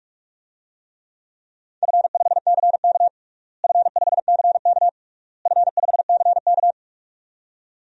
Morse code, heard as a keyed tone, about 43 wpm, 700 Hz. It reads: F5CK F5CK F5CK
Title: DXCC-100 Call Signs - 4 Characters - Repeated 3x - 45wpm